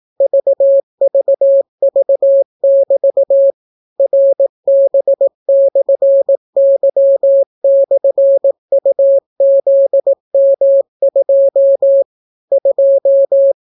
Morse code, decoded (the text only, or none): VVV<BT> RB/Y/UZM2 2